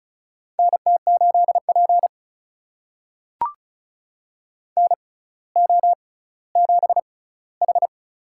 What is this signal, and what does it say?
Morse code, heard as a keyed tone, about 34 wpm, spaced as for 18 wpm, 700 Hz. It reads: NT8P NO7H
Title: USA Call Signs - 4 Characters - 6x Character Spacing - 35wpm